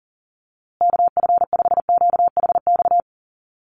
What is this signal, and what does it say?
Morse code, heard as a keyed tone, about 40 wpm, 700 Hz. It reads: KF5QHX
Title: USA Call Signs Repeated 2x 40wpm